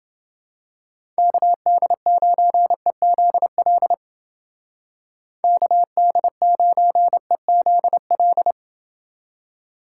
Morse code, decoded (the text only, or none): KD9EZL KD9EZL